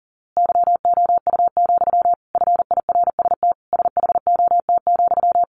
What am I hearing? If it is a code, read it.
Morse code, heard as a keyed tone, about 40 wpm, 700 Hz. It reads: YOU, FIRST SHOT,
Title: A Sound Of Thunder 40wpm